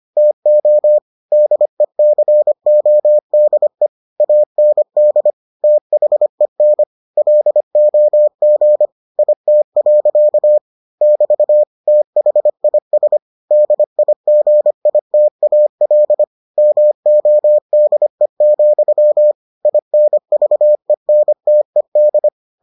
Morse code, heard as a keyed tone, about 25 wpm, 600 Hz. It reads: TO DECODE AND THEN LOG IT. <BT> THIS DIGITAL MODE, INVENTED